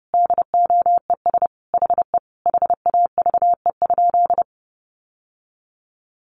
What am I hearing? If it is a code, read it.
Morse code, heard as a keyed tone, about 30 wpm, 700 Hz. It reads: DOES HE HAVE?
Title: Sentences from Top 1000 Words 30wpm